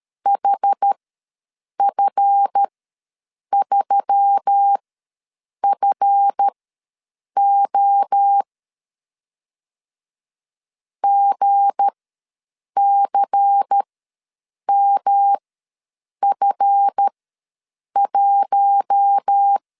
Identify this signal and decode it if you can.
Morse code, heard as a keyed tone, about 13 wpm, spaced as for 7 wpm, 800 Hz. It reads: HF3FO GCMF1
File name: LEZIONI / 7 GF9Z / 7 GF ws 12wpm 3140